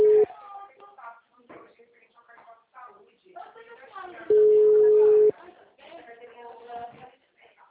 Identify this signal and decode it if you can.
Ringback tone (Europe, 425 Hz), with speech.